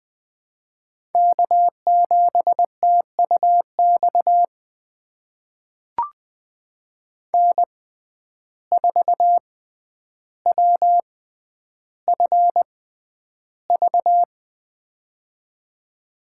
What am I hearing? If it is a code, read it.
Morse code, heard as a keyed tone, about 20 wpm, spaced as for 10 wpm, 700 Hz. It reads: K7TUX N4WFV
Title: USA Call Signs - 5 Characters - 6x Character Spacing - 20wpm